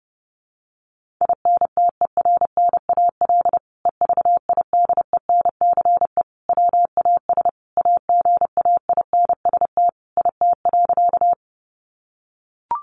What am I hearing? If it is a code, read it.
Morse code, heard as a keyed tone, about 30 wpm, 700 Hz. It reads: INTERNAL EVIDENCE WAS AGAINST IT.
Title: Sentences from Top 1400 Words 30wpm